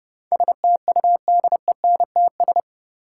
Morse code, heard as a keyed tone, about 32 wpm, 700 Hz. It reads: STUDENTS
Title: Sets of 5 Words Encore - New Word Drill - 30wpm